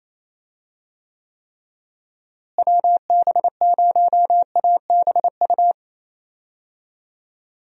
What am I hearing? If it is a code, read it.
Morse code, heard as a keyed tone, about 28 wpm, 700 Hz. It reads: WB0ABU
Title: USA Call Signs - 6 Characters - Repeated 3x - 28wpm